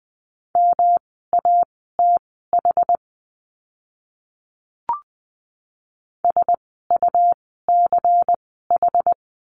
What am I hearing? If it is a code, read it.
Morse code, heard as a keyed tone, about 20 wpm, spaced as for 15 wpm, 700 Hz. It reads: MATH SUCH
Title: 108 Lesson 18f - Letter m - Words 20wpm